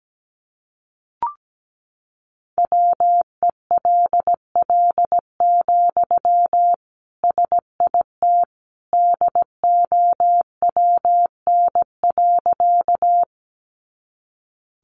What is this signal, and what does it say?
Morse code, heard as a keyed tone, about 17 wpm, 700 Hz. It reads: WELL, SIT DOWN.
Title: Sentences from Top 500 Words 17wpm